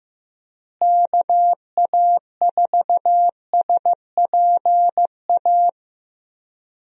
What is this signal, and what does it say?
Morse code, heard as a keyed tone, about 15 wpm, 700 Hz. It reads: KA4SPA
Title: USA Call Signs - 6 Characters - Repeated 2x - 15wpm